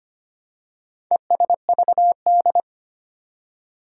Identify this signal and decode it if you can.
Morse code, heard as a keyed tone, about 25 wpm, 700 Hz. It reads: ESVD
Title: Four Challenging Letter-Number Combinations 25wpm